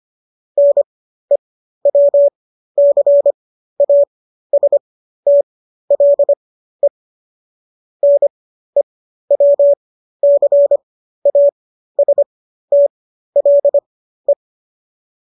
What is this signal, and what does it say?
Morse code, heard as a keyed tone, about 25 wpm, spaced as for 13 wpm, 570 Hz. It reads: NEWCASTLE NEWCASTLE
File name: qso207 13